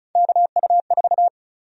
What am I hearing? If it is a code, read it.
Morse code, heard as a keyed tone, about 34 wpm, 700 Hz. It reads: KU4
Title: USA Call Sign Prefixes - Repeated 2x - 35wpm